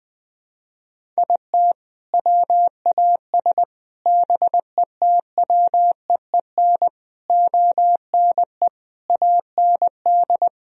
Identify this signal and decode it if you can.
Morse code, heard as a keyed tone, about 21 wpm, 700 Hz. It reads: IT WAS BETWEEN ONE AND
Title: Sentences from Top 1000 Words - Review Part 1 - 20wpm